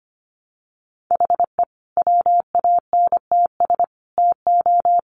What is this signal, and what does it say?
Morse code, heard as a keyed tone, about 25 wpm, 700 Hz. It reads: HE WANTS TO
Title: Additional Easy Phrases 25wpm